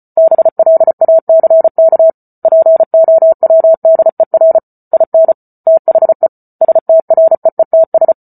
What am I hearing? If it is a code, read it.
Morse code, heard as a keyed tone, about 34 wpm, 650 Hz. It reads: BLACK POWDER IN THE STREETS